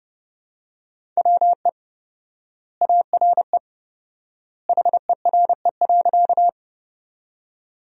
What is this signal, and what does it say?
Morse code, heard as a keyed tone, about 30 wpm, 700 Hz. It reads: WE ARE HERE.
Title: 4x Word-Spacing - Sentences from Top 200 Words - 30wpm